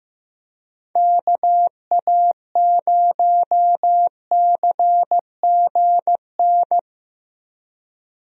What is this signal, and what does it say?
Morse code, heard as a keyed tone, about 16 wpm, 700 Hz. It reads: KA0CGN
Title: USA Call Signs - 6 Characters - Repeated 3x - 15wpm